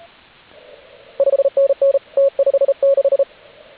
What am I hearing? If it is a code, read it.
Morse code, heard as a keyed tone, about 33 wpm, 550 Hz, with radio noise. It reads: 5NN T56
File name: AUDIO2019 / HG8DX CW